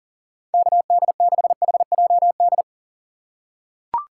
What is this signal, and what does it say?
Morse code, heard as a keyed tone, about 37 wpm, 700 Hz. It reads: KD6HJD
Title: USA Call Signs - 6 Characters - 6x Character Spacing - 40wpm